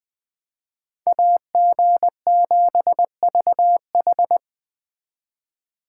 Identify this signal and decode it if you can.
Morse code, heard as a keyed tone, about 20 wpm, 700 Hz. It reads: AG7VH
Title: Speed Racing - USA 2x2 Call Signs 17wpm